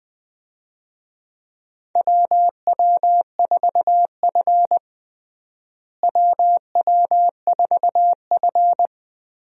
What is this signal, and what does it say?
Morse code, heard as a keyed tone, about 20 wpm, 700 Hz. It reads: WW4F WW4F
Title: USA Call Signs - 4 Characters - Repeated 3x - 20wpm